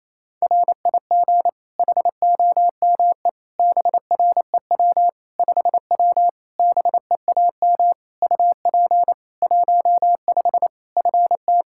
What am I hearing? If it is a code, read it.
Morse code, heard as a keyed tone, about 27 wpm, 700 Hz. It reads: RIG HOME BREW 5W BEAM UP 15 FT
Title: K1USN SST QSO Practice 28wpm